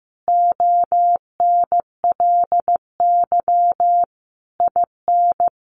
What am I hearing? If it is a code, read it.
Morse code, heard as a keyed tone, about 15 wpm, 700 Hz. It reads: ONLY IN
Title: Sentences from Top 1300 Words 15wpm